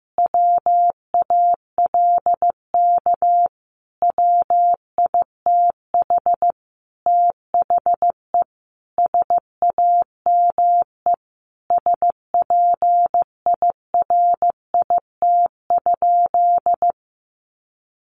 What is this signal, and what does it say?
Morse code, heard as a keyed tone, about 15 wpm, 700 Hz. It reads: WALK WITH THE SAME SPIRIT?
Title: Sentences from Top 1400 Words 15wpm